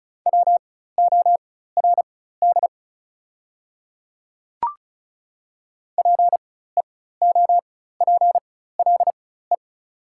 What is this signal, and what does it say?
Morse code, heard as a keyed tone, about 36 wpm, spaced as for 16 wpm, 690 Hz. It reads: WORD PEOPLE
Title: 4x Character Spacing - Top 300 Words - 35wpm